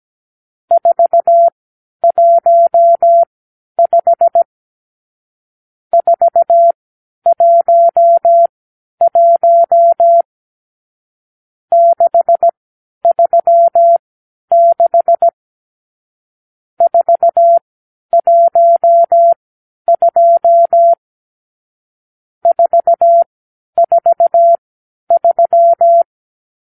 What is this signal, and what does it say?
Morse code, heard as a keyed tone, about 17 wpm, spaced as for 10 wpm, 680 Hz. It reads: 415 411 636 412 443
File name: k7qo / Lesson 50 (1-6)